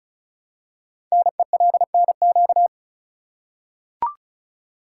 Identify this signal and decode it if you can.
Morse code, heard as a keyed tone, about 36 wpm, 700 Hz. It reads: NELNQ
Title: Code Groups - 5 Letters - 35wpm